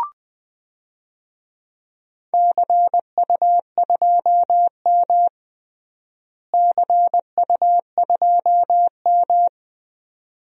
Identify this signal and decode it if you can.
Morse code, heard as a keyed tone, about 20 wpm, 700 Hz. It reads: CU2M CU2M
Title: DXCC-100 Call Signs - 4 Characters - Repeated 2x - 20wpm